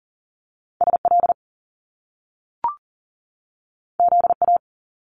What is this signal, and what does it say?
Morse code, heard as a keyed tone, about 42 wpm, 700 Hz. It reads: SL ZA
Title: Top Two Letter Combinations in Words 40wpm